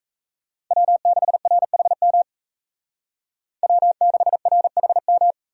Morse code, heard as a keyed tone, about 36 wpm, 700 Hz. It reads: W6RHM W6RHM